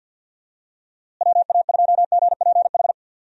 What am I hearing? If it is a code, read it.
Morse code, heard as a keyed tone, about 52 wpm, 700 Hz. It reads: WA2GPH
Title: USA Call Signs - 6 Characters - Repeated 3x - 50wpm